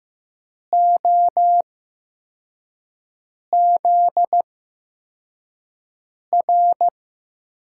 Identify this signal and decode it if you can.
Morse code, heard as a keyed tone, about 15 wpm, spaced as for 4 wpm, 700 Hz. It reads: OZR
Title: USA Call Signs - 6 Characters - 8x Character Spacing - 15wpm